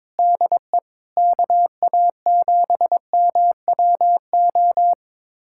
Morse code, heard as a keyed tone, about 22 wpm, 700 Hz. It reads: DE KA7MWO
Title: USA Callsign DE Callsign - 6 Characters - 22wpm